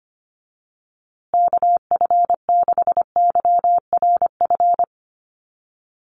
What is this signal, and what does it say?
Morse code, heard as a keyed tone, about 25 wpm, 700 Hz. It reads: KF6YRF
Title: USA Call Signs Repeated 2x 25wpm